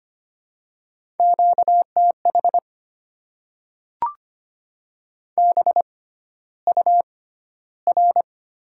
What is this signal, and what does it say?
Morse code, heard as a keyed tone, about 25 wpm, spaced as for 13 wpm, 690 Hz. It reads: QTH BUR
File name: Top 90 QSO Elements - 6x Character Spacing - 25wpm